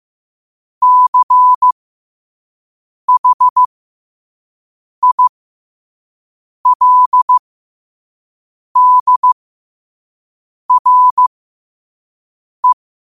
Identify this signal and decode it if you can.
Morse code, heard as a keyed tone, about 15 wpm, spaced as for 5 wpm, 1000 Hz. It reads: CHILDRE
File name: 5wpm